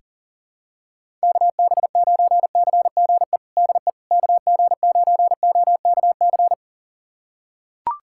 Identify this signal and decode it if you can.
Morse code, heard as a keyed tone, about 37 wpm, 700 Hz. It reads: KB9CGE DE KG9OKC
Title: USA Callsign DE Callsign - 6 Characters - 40wpm